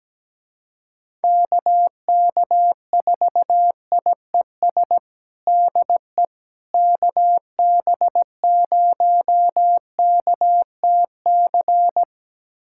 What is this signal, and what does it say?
Morse code, heard as a keyed tone, about 17 wpm, 700 Hz. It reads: KK4IES DE KB0KTC
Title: USA Callsign DE Callsign 17wpm